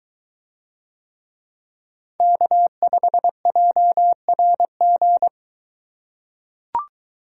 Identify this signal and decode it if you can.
Morse code, heard as a keyed tone, about 23 wpm, 700 Hz. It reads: K5JRG